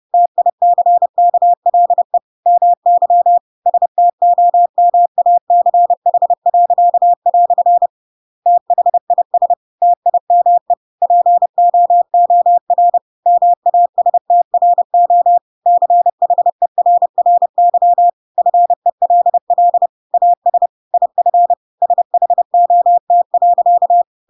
Morse code, heard as a keyed tone, about 30 wpm, 700 Hz. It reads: TICKLE MY STOMACH." THIS TIME POOR MASTRO CHERRY FELL AS IF SHOT.
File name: Pinocchio / Pinocchio ch1 30 30 ik1hge